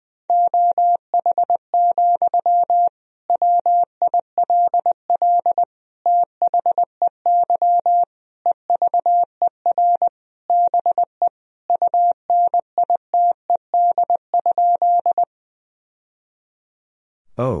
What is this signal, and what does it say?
Morse code, heard as a keyed tone, about 20 wpm, 700 Hz. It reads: OH, WILL THEY EVER BE UNITED?
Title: Sentences from Top 1000 Words - Review Part 4 - 20wpm